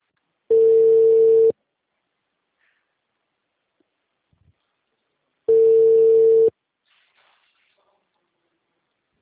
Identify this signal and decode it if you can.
Ringback tone (China, 450 Hz).